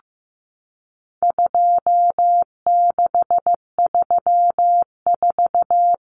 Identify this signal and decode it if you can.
Morse code, heard as a keyed tone, about 15 wpm, 700 Hz. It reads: 2634
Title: Numbers - 4-Digit 15wpm